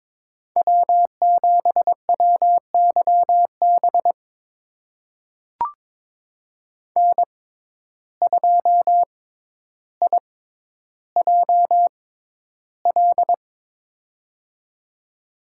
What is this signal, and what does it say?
Morse code, heard as a keyed tone, about 22 wpm, spaced as for 11 wpm, 700 Hz. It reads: W7WYB N2IJL
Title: USA Call Signs - 5 Characters - 6x Character Spacing - 22wpm